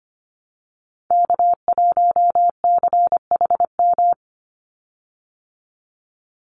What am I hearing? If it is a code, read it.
Morse code, heard as a keyed tone, about 25 wpm, 700 Hz. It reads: K1CHM
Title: Call Signs Repeated 3x - 25wpm